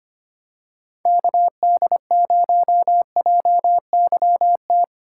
Morse code, heard as a keyed tone, about 25 wpm, 700 Hz. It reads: KD0JYT
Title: USA Call Signs - 6 Characters - 6x Character Spacing - 25wpm